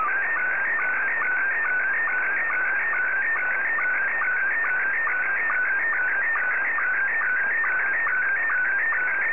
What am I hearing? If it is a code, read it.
SSTV (slow-scan television), Scottie 1.